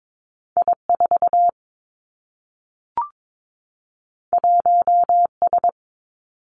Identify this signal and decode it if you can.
Morse code, heard as a keyed tone, about 22 wpm, 700 Hz. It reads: I 4 1 S
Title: Two Challenging Letter-Number Combinations 22wpm